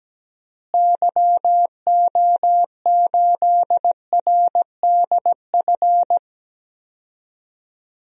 Morse code, heard as a keyed tone, about 17 wpm, 700 Hz. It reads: YO8RDF
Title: DXCC-100 Call Signs - 6 Characters - Repeated 2x - 17wpm